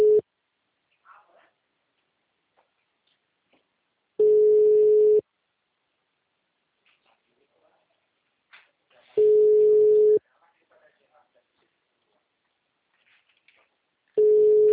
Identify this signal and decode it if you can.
Ringback tone (Europe, 425 Hz), with speech.